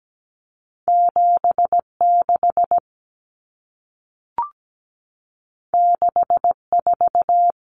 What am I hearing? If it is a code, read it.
Morse code, heard as a keyed tone, about 17 wpm, 700 Hz. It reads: 76 64
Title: Numbers - 2-Digit 17wpm